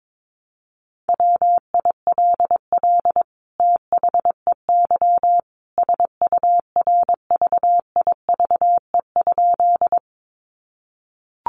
Morse code, heard as a keyed tone, about 22 wpm, 700 Hz. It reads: WILL THEY SURVIVE?
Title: Sentences from Top 1300 Words 22wpm